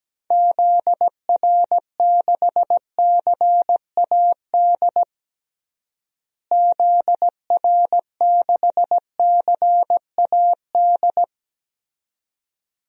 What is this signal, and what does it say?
Morse code, heard as a keyed tone, about 17 wpm, 700 Hz. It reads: ZR6CAD ZR6CAD
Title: DXCC-100 Call Signs - 6 Characters - Repeated 2x - 17wpm